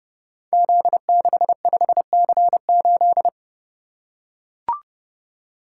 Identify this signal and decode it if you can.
Morse code, heard as a keyed tone, about 29 wpm, 700 Hz. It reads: Z65C8
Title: Code Groups - 5 Letter:Numbers - 30wpm